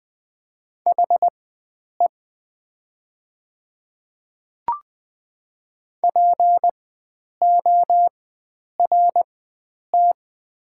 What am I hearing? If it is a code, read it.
Morse code, heard as a keyed tone, about 20 wpm, spaced as for 9 wpm, 700 Hz. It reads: HE PORT
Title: 4x Character Spacing - Top 300 Words - 20wpm